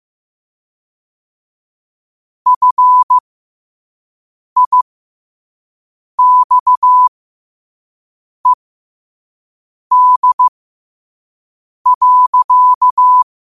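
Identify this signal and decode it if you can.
Morse code, heard as a keyed tone, about 15 wpm, spaced as for 5 wpm, 1000 Hz. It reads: FIXED.
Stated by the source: Morse code quote of the day sent at 5 wpm.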